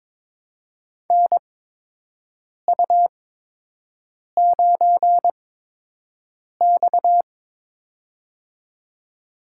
Morse code, heard as a keyed tone, about 22 wpm, spaced as for 6 wpm, 700 Hz. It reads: NU9X
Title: USA Call Signs - 4 Characters - 8x Character Spacing - 22wpm